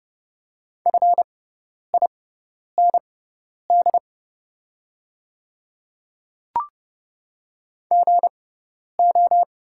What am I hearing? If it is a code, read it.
Morse code, heard as a keyed tone, about 30 wpm, spaced as for 10 wpm, 700 Hz. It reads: FIND GO
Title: 6x Character Spacing - Top 100 Words - 30wpm